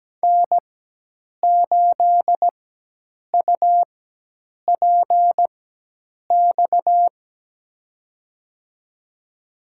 Morse code, heard as a keyed tone, about 17 wpm, spaced as for 8 wpm, 700 Hz. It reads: N8UPX
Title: USA Call Signs - 4x Character Spacing - 17wpm